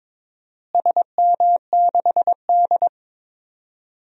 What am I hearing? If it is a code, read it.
Morse code, heard as a keyed tone, about 22 wpm, 700 Hz. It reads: SM6D
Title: DXCC-100 Call Signs - 4 Characters - Repeated 2x - 22wpm